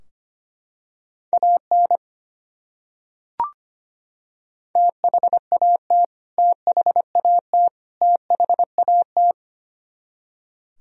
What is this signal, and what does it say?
Morse code, heard as a keyed tone, about 25 wpm, 700 Hz. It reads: AN THAT THAT THAT
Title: Top 100 Words Repeated 25wpm